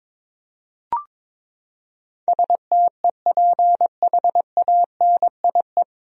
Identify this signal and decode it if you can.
Morse code, heard as a keyed tone, about 22 wpm, 700 Hz. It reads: STEPHANIE
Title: USA Common Names - 22wpm